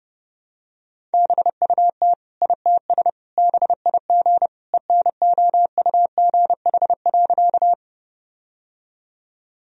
Morse code, heard as a keyed tone, about 32 wpm, 700 Hz. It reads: BUT ITS BIG ENOUGH.
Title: Sentences from Top 1000 Words - Review Part 1 - 30wpm